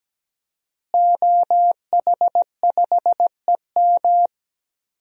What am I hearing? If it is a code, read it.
Morse code, heard as a keyed tone, about 17 wpm, 700 Hz. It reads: OH5EM
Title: DXCC-100 Call Signs - 5 Characters - Repeated 3x - 17wpm